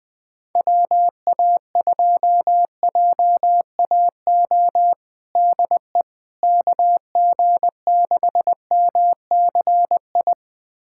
Morse code, heard as a keyed tone, about 21 wpm, 700 Hz. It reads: WA2JAO DE KG6MCI
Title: USA Callsign DE Callsign 20wpm